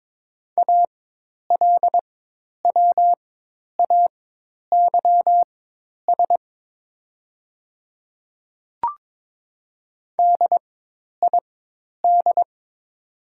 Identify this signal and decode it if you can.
Morse code, heard as a keyed tone, about 22 wpm, spaced as for 10 wpm, 700 Hz. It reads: ALWAYS DID
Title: 4x Character Spacing - Top 500 Words - 22wpm